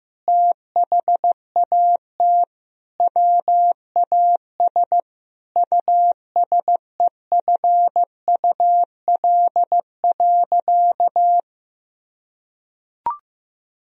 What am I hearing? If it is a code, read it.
Morse code, heard as a keyed tone, about 15 wpm, 700 Hz. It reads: THAT WAS USEFUL.
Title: Sentences from Top 1000 Words - Review Part 4 - 15wpm